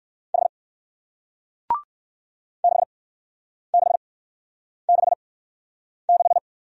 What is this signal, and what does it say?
Morse code, heard as a keyed tone, about 57 wpm, spaced as for 9 wpm, 700 Hz. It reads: H 6666